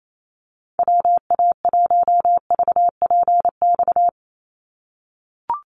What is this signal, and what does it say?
Morse code, heard as a keyed tone, about 28 wpm, 700 Hz. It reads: WA1VPX
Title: USA Call Signs 28wpm